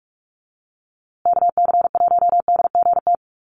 Morse code, heard as a keyed tone, about 46 wpm, 700 Hz. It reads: KC1DGT
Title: USA Call Signs Repeated 3x 45wpm